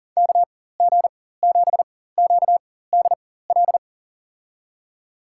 Morse code, heard as a keyed tone, about 40 wpm, spaced as for 19 wpm, 700 Hz. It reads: KG7QDL
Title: USA Call Signs - 6 Characters - 4x Character Spacing - 40wpm